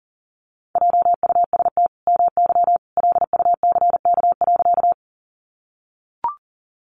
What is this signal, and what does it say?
Morse code, heard as a keyed tone, about 40 wpm, 700 Hz. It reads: JUST MY LUCK.
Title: Sentences from Top 1500 Words 40wpm